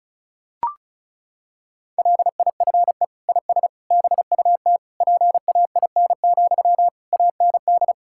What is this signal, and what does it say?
Morse code, heard as a keyed tone, about 36 wpm, 700 Hz. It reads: LIFE IS BUT PAIN, AND
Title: Sentences from Top 1600 Words 35wpm